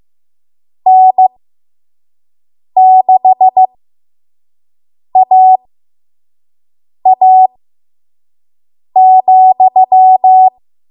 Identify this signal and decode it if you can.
Morse code, heard as a keyed tone, about 15 wpm, spaced as for 5 wpm, 750 Hz. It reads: N6AA,
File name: Morse / 05WPM